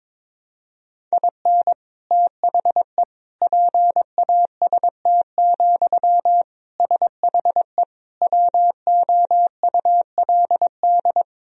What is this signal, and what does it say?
Morse code, heard as a keyed tone, about 22 wpm, 700 Hz. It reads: IN THE PAST, SHE WOULD
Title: Sentences from Top 1000 Words - Review Part 3 - 22wpm